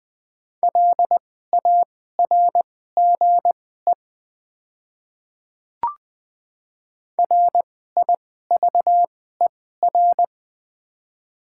Morse code, heard as a keyed tone, about 20 wpm, spaced as for 14 wpm, 700 Hz. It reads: LARGE RIVER
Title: 2x Character Spacing - Top 500 Words - 20wpm